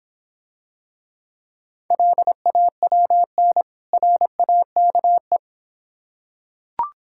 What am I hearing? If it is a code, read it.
Morse code, heard as a keyed tone, about 26 wpm, 700 Hz. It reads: LAWN RAKE